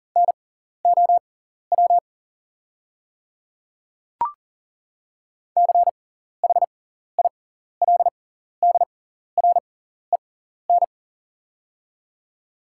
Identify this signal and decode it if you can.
Morse code, heard as a keyed tone, about 39 wpm, spaced as for 14 wpm, 690 Hz. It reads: NOW CHILDREN
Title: 6x Character Spacing - Top 300 Words - 40wpm